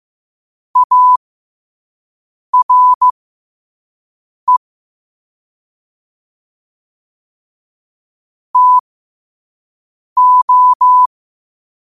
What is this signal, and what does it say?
Morse code, heard as a keyed tone, about 15 wpm, spaced as for 5 wpm, 1000 Hz. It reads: ARE TO
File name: qotdfiles / 5wpm